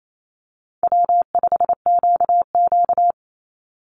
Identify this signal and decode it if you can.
Morse code, heard as a keyed tone, about 28 wpm, 700 Hz. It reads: W5QQ
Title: Call Signs Repeated 2x - 28wpm